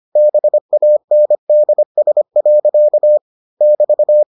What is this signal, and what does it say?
Morse code, heard as a keyed tone, about 25 wpm, 600 Hz. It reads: BANDS. <BT>